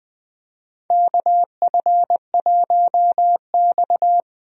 Morse code, heard as a keyed tone, about 20 wpm, 700 Hz. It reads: KF1X
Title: USA Call Signs - 4 Characters - 20wpm